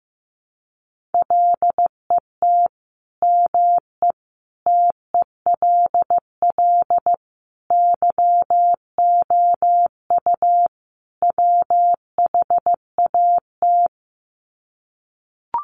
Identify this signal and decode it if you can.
Morse code, heard as a keyed tone, about 15 wpm, 700 Hz. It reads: LET ME TELL YOU WHAT
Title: Sets of 5 Words Encore 15wpm